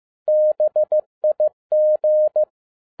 Morse code, heard as a keyed tone, about 15 wpm, 610 Hz. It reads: BIG